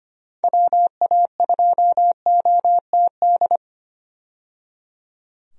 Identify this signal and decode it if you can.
Morse code, heard as a keyed tone, about 25 wpm, 700 Hz. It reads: WA2OTD
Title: USA Call Signs - 6 Characters - 25wpm